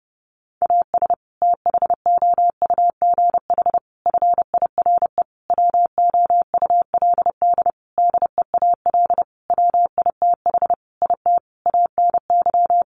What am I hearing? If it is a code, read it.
Morse code, heard as a keyed tone, about 30 wpm, 700 Hz. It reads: AS THOUGH FIRE WOULD DEAL WITH IT ANY
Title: Sentences from Top 1500 Words 30wpm